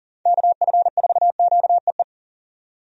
Morse code, heard as a keyed tone, about 38 wpm, 700 Hz. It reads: KF4QEE
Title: USA Call Signs - 6 Characters - 40wpm